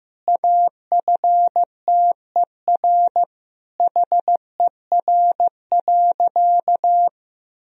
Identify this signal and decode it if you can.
Morse code, heard as a keyed tone, about 15 wpm, 700 Hz. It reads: AFTER HER.
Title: Sentences from Top 1000 Words - Review Part 3 - 15wpm